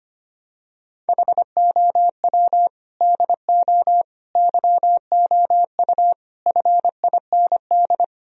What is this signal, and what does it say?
Morse code, heard as a keyed tone, about 25 wpm, 700 Hz. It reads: HOW DO YOU FIND
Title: Sentences from Top 1000 Words - Review Part 4 - 25wpm